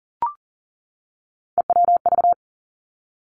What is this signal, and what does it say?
Morse code, heard as a keyed tone, about 40 wpm, 700 Hz. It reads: EWV
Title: Three Challenging Letter-Number Combinations 40wpm